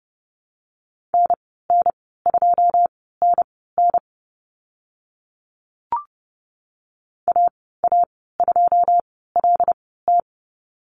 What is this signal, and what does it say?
Morse code, heard as a keyed tone, about 30 wpm, spaced as for 17 wpm, 700 Hz. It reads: NN2NN AA2LT
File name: Lesson 14g - Number 2 - Call Signs 30wpm